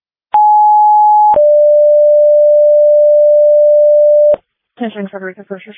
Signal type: two-tone page